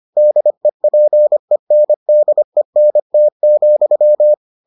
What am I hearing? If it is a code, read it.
Morse code, heard as a keyed tone, about 25 wpm, 600 Hz. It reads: DEPENDENT,